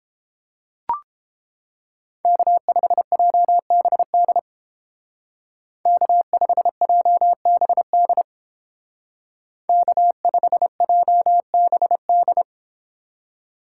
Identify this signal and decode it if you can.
Morse code, heard as a keyed tone, about 29 wpm, 700 Hz. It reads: K5JBD K5JBD K5JBD